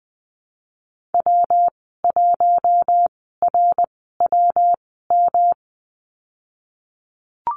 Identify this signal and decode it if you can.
Morse code, heard as a keyed tone, about 20 wpm, spaced as for 15 wpm, 700 Hz. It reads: W1RWM
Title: 128 Lesson 21g - Number 6 - Call Signs 20wpm